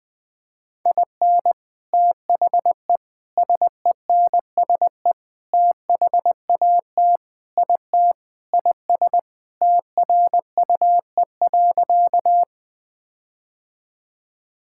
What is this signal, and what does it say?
Morse code, heard as a keyed tone, about 20 wpm, 700 Hz. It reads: IN THE SENSE THAT IT IS TRUE.
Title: Sentences with 4-Word Phrases - Spoken Beginning - 20wpm